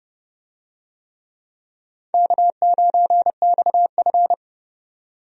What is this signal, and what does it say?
Morse code, heard as a keyed tone, about 30 wpm, 700 Hz. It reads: K9XF
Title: USA Call Signs - 4 Characters - 30wpm